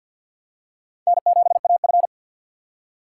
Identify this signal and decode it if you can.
Morse code, heard as a keyed tone, about 52 wpm, 700 Hz. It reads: N7AF
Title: USA Call Signs - 4 Characters - 6x Character Spacing - 50wpm